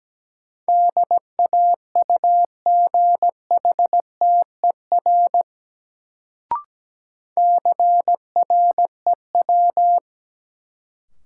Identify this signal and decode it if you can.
Morse code, heard as a keyed tone, about 17 wpm, 700 Hz. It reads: DAUGHTER CREW
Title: Sentences from Top 1500 Words - New Word Drill - 17wpm